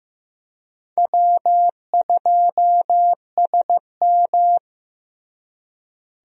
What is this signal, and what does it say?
Morse code, heard as a keyed tone, about 15 wpm, 700 Hz. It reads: W2SM
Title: USA Call Signs - 4 Characters - Repeated 3x - 15wpm